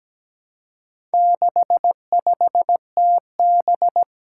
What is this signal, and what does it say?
Morse code, heard as a keyed tone, about 17 wpm, 700 Hz. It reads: 65TB
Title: Four Challenging Letter-Number Combinations 17wpm